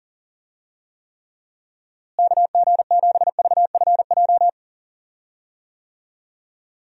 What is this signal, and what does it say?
Morse code, heard as a keyed tone, about 40 wpm, 700 Hz. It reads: KG7VFJ
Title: USA Call Signs - 6 Characters - Repeated 3x - 40wpm